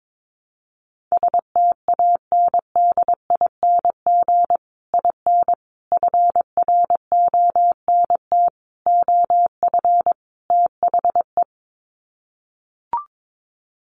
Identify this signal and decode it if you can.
Morse code, heard as a keyed tone, about 22 wpm, 700 Hz. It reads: STANDING IN FRONT OF THE
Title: Sets of 5 Words Encore2 22wpm